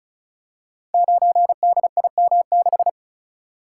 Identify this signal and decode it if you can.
Morse code, heard as a keyed tone, about 36 wpm, 700 Hz. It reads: 9DIM6
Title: Code Groups - 5 Letter:Numbers - 35wpm